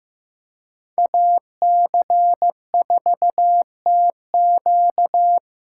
Morse code, heard as a keyed tone, about 15 wpm, 700 Hz. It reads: AC4TQ
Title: USA Call Signs - 2x Character Spacing - 15wpm